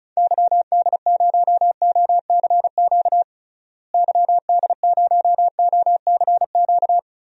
Morse code, heard as a keyed tone, about 36 wpm, 700 Hz. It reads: YD0OCQ YD0OCQ
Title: DXCC-100 Call Signs - 6 Characters - Repeated 3x - 35wpm